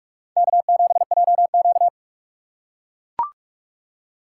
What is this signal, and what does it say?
Morse code, heard as a keyed tone, about 42 wpm, 700 Hz. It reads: K7JQ